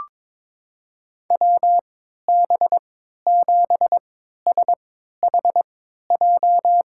Morse code, heard as a keyed tone, about 22 wpm, spaced as for 13 wpm, 700 Hz. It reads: WB7SHJ
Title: USA Call Signs - 6 Characters - 3x Character Spacing - 22wpm